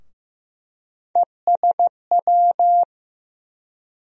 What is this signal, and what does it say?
Morse code, heard as a keyed tone, about 15 wpm, 700 Hz. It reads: ESW
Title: USA Call Sign Postfixes - 3 Letters 15wpm